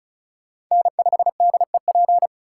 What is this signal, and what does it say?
Morse code, heard as a keyed tone, about 42 wpm, 700 Hz. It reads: N5DEP
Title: Speed Racing - USA 1x3 Call Signs 35wpm